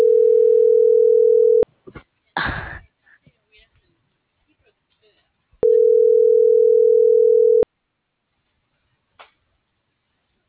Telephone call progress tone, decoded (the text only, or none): ringback tone